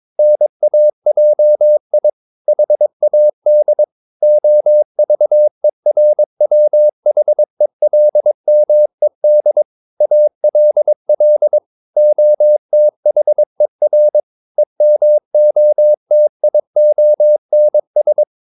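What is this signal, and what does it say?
Morse code, heard as a keyed tone, about 22 wpm, 600 Hz. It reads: NAJI HAD OVERWHELMED ALL OTHER EMOTIONS
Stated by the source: Morse code practice at 22 wpm, sending text from an English novel.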